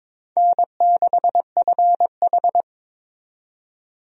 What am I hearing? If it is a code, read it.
Morse code, heard as a keyed tone, about 22 wpm, 700 Hz. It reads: N6FH
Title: USA Call Signs - 4 Characters - 22wpm